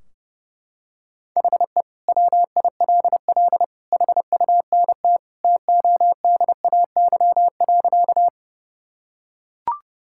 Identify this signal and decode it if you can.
Morse code, heard as a keyed tone, about 32 wpm, 700 Hz. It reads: HE WILL HUNT TODAY.
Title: Sentences from Top 1600 Words 30wpm